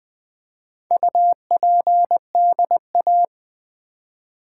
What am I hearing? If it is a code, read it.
Morse code, heard as a keyed tone, about 20 wpm, 700 Hz. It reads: UPDA
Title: Build A Word - 2 Letters At A Time - 6-Letter Words - 20wpm